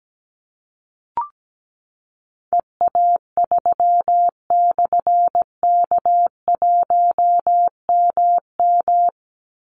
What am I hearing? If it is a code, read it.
Morse code, heard as a keyed tone, about 17 wpm, 700 Hz. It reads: EA3/K1MM
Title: Call Signs - 17wpm